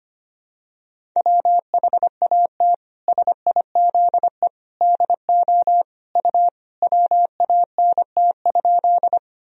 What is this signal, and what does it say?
Morse code, heard as a keyed tone, about 25 wpm, 700 Hz. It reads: WHAT SIZE DO U WANT?
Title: Sentences from Top 400 Words and Abbreviations 25wpm